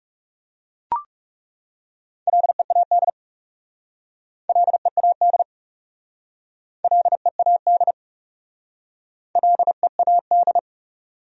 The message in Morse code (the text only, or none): LEAD LEAD LEAD LEAD